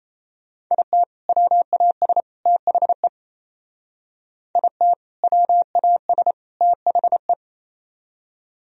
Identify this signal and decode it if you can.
Morse code, heard as a keyed tone, about 36 wpm, 700 Hz. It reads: IT WAS THE IT WAS THE